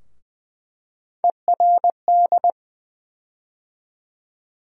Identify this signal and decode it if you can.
Morse code, heard as a keyed tone, about 20 wpm, 700 Hz. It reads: ERD
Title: USA Call Sign Postfixes - 3 Letters 20wpm